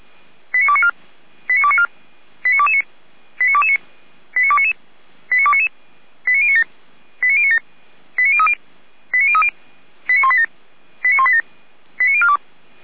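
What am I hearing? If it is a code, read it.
Five-tone selective call (ZVEI), heard as quick runs of five tones, tones of about 70 ms. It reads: 89294 89294 89229 89229 89299 89299 89097 89097 89030 89030 89187 89187 89042
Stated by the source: SDR reception of ZVEI five-tone call 89276, from the feuerwehr-orlamuende.org fire brigade server.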